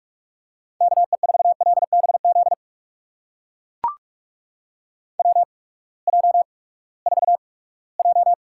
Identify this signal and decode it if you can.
Morse code, heard as a keyed tone, about 44 wpm, spaced as for 22 wpm, 700 Hz. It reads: KE4LBZ WJ4J
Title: USA Call Signs - 8x Character Spacing - 45wpm